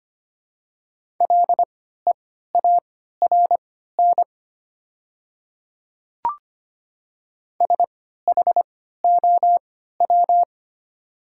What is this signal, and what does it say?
Morse code, heard as a keyed tone, about 25 wpm, spaced as for 14 wpm, 700 Hz. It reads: LEARN SHOW
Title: 3x Character Spacing - Top 300 Words - 25wpm